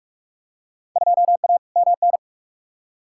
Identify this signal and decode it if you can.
Morse code, heard as a keyed tone, about 46 wpm, 700 Hz. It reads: 2A MN
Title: Field Day Exchanges - 45wpm